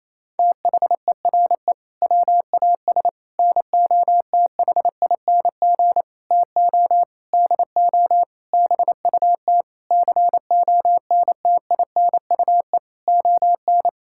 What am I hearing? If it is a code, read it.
Morse code, heard as a keyed tone, about 28 wpm, 700 Hz. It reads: THERE WAS NOTHING TO DO BUT CONTINUE ON
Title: Sentences from Top 1000 Words - Review Part 6 - 28wpm